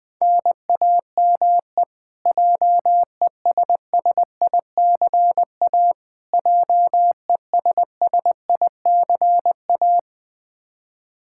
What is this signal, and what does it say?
Morse code, heard as a keyed tone, about 20 wpm, 700 Hz. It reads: NAME JESSICA JESSICA
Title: Practice QSO - Exchange 1 - Beginning - 20wpm